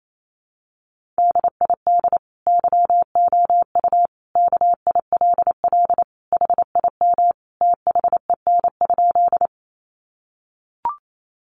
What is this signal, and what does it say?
Morse code, heard as a keyed tone, about 28 wpm, 700 Hz. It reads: DID YOU KILL HIM THEN?
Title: Sentences from Top 1400 Words 28wpm